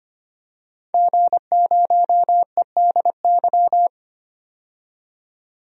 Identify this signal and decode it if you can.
Morse code, heard as a keyed tone, about 25 wpm, 700 Hz. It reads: G0EDY
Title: DXCC-100 Call Signs - 5 Characters - 25wpm